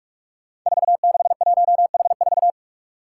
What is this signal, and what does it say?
Morse code, heard as a keyed tone, about 44 wpm, 700 Hz. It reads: 461H4
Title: Five Challenging Letter-Number Combinations 45wpm